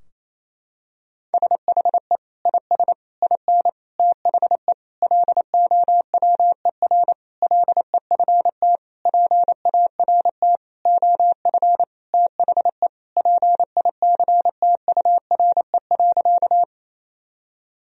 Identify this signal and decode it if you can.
Morse code, heard as a keyed tone, about 28 wpm, 700 Hz. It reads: SHE IS IN THE LOWER LEFT PART OF THE PICTURE.
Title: Sentences from Top 1000 Words - Review Part 4 - 28wpm